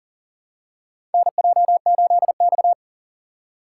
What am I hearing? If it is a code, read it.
Morse code, heard as a keyed tone, about 38 wpm, 700 Hz. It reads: NJ8X
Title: USA Call Signs - 4 Characters - 6x Character Spacing - 40wpm